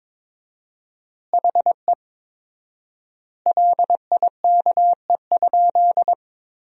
Morse code, heard as a keyed tone, about 22 wpm, 700 Hz. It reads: HE LIKE?
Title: 4x Word-Spacing - Sentences from Top 100 Words - 22wpm